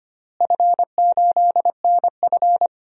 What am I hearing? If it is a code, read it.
Morse code, heard as a keyed tone, about 25 wpm, 700 Hz. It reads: F8NF
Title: DXCC-100 Call Signs - 4 Characters - Repeated 3x - 25wpm